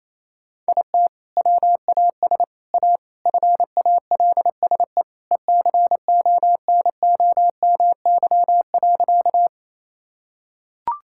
Morse code, heard as a keyed tone, about 28 wpm, 700 Hz. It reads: IT WAS A FALSE ECONOMY.
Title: Sentences from Top 1600 Words 28wpm